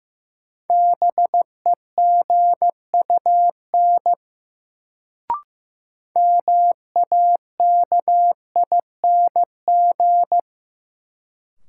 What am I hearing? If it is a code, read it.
Morse code, heard as a keyed tone, about 15 wpm, 700 Hz. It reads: BEGUN MAKING
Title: Sentences from Top 1000 Words - Review Part 6 - New Word Drill - 15wpm